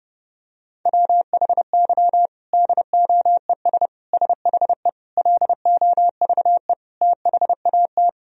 Morse code, heard as a keyed tone, about 30 wpm, 700 Hz. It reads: WHY DOES SHE LOVE THAT
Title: Sentences from Top 1000 Words - Review Part 4 - 30wpm